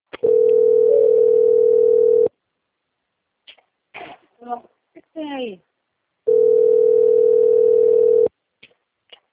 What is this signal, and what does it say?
Ringback tone (North America), 440 + 480 Hz, with speech.
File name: out- - - - - .1398